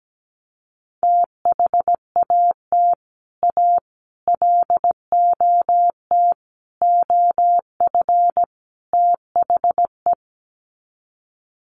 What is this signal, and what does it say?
Morse code, heard as a keyed tone, about 17 wpm, 700 Hz. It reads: THAT A LOT OF THE
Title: Sets of 5 Words Encore2 17wpm